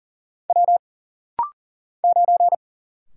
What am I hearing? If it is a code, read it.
Morse code, heard as a keyed tone, about 40 wpm, 700 Hz. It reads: W 9
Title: Single Letter-Number - Rapid-Fire 40wpm